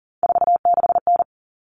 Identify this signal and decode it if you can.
Morse code, heard as a keyed tone, about 40 wpm, 700 Hz. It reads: 46N
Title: All Cut Numbers - 3-Digit 40wpm